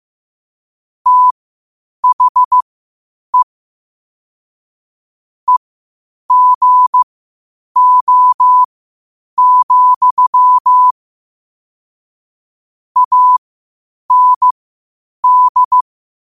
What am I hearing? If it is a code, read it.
Morse code, heard as a keyed tone, about 15 wpm, spaced as for 8 wpm, 1000 Hz. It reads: THE EGO, AND
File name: qotdfiles / 8wpm